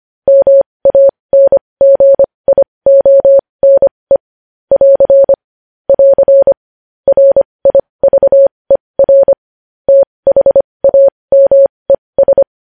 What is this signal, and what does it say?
Morse code, heard as a keyed tone, about 25 wpm, spaced as for 20 wpm, 550 Hz. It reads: MANGIONE <AR> <AR> RIVER THAMES
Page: news-headlines-in-morse-code-at-20-wpm